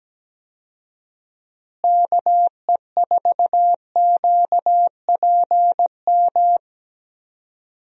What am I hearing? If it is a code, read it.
Morse code, heard as a keyed tone, about 17 wpm, 700 Hz. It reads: KE4QPM
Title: USA Call Signs - 6 Characters - Repeated 3x - 17wpm